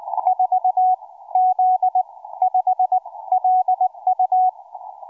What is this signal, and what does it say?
Morse code, heard as a keyed tone, about 20 wpm, spaced as for 15 wpm, 740 Hz, with radio noise. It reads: 4Z5LU
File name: cw forum / 4z5lu 20m 19h25